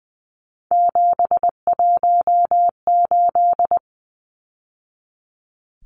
Morse code, heard as a keyed tone, about 20 wpm, 700 Hz. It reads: 718
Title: Numbers - 3-Digit 20wpm